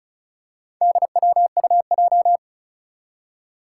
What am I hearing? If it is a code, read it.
Morse code, heard as a keyed tone, about 36 wpm, 700 Hz. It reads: DWUJ
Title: Four Challenging Letter-Number Combinations 35wpm